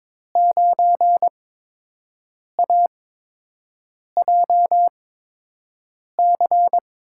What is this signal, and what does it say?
Morse code, heard as a keyed tone, about 22 wpm, spaced as for 6 wpm, 700 Hz. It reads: 9AJC
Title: USA Call Signs - 5 Characters - 8x Character Spacing - 22wpm